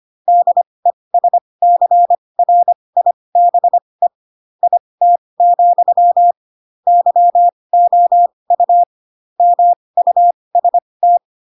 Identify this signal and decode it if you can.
Morse code, heard as a keyed tone, about 25 wpm, spaced as for 20 wpm, 700 Hz. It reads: DESCRIBE IT, YOU MUST